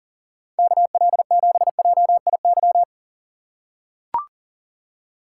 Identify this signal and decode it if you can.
Morse code, heard as a keyed tone, about 39 wpm, 700 Hz. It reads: KL7JIY
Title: USA Call Signs - 6 Characters - 3x Character Spacing - 40wpm